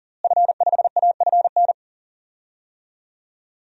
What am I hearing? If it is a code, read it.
Morse code, heard as a keyed tone, about 37 wpm, 700 Hz. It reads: F5AFN
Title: DXCC-100 Call Signs - 5 Characters - Repeated 3x - 40wpm